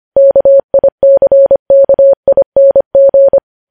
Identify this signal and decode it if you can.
Morse code, heard as a keyed tone, about 25 wpm, 550 Hz. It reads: KICKING
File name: podcast / News headlines-1x-25x25